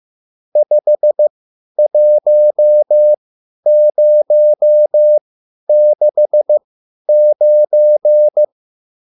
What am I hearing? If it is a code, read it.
Morse code, heard as a keyed tone, about 15 wpm, spaced as for 10 wpm, 600 Hz. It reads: 51069